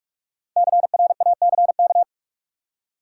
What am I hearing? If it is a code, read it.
Morse code, heard as a keyed tone, about 46 wpm, 700 Hz. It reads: CRACK
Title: Five Letter Words - 45wpm